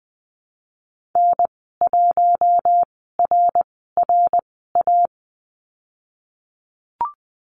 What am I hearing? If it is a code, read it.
Morse code, heard as a keyed tone, about 19 wpm, spaced as for 14 wpm, 700 Hz. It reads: N1RRA
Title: 055 Lesson 10g - Letter r - Call Signs 20wpm